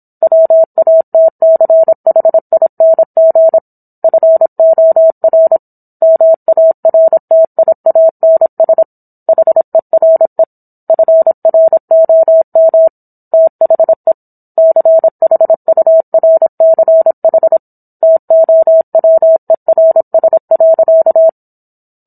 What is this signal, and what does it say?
Morse code, heard as a keyed tone, about 27 wpm, 650 Hz. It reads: WATCHING FOR MARTIANS HERE FROM THE CHURCH TOWERS.